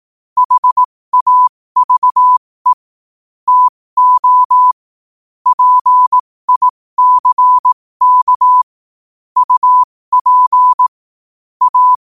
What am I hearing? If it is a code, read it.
Morse code, heard as a keyed tone, about 18 wpm, spaced as for 15 wpm, 1000 Hz. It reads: HAVE TO PICK UP A
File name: qotdfiles / 15wpm